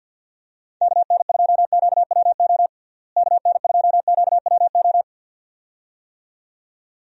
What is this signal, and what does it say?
Morse code, heard as a keyed tone, about 50 wpm, 700 Hz. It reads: KN2QWO KN2QWO
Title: USA Call Signs - 6 Characters - Repeated 2x - 50wpm